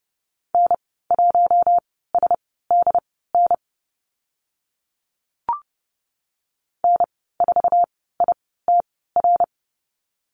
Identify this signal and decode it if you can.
Morse code, heard as a keyed tone, about 30 wpm, spaced as for 17 wpm, 700 Hz. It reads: N1SDN N4ITR
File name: Lesson 14g - Number 2 - Call Signs 30wpm